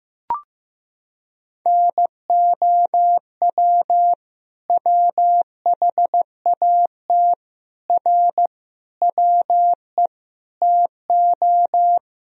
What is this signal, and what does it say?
Morse code, heard as a keyed tone, about 15 wpm, 700 Hz. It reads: NOW WHAT R WE TO
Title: Sentences from Top 100 Words and Abbreviations 15wpm